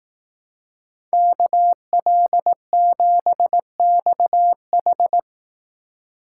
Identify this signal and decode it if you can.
Morse code, heard as a keyed tone, about 18 wpm, 700 Hz. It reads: KL7XH
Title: Speed Racing - USA 2x2 Call Signs 15wpm